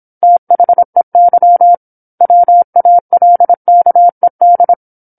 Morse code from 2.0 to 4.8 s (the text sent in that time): WALKED